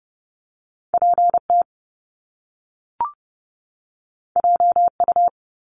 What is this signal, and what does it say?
Morse code, heard as a keyed tone, about 30 wpm, 700 Hz. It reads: PT JU
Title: Top Two Letter Combinations in Words 30wpm